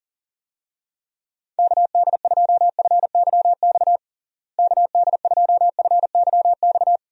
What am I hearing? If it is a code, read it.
Morse code, heard as a keyed tone, about 37 wpm, 700 Hz. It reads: KD2FYX KD2FYX
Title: USA Call Signs - 6 Characters - Repeated 2x - 40wpm